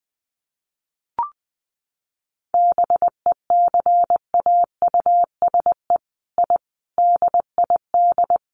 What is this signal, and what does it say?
Morse code, heard as a keyed tone, about 20 wpm, 700 Hz. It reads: BECAUSE I DID
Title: Sets of 5 Words Encore2 20wpm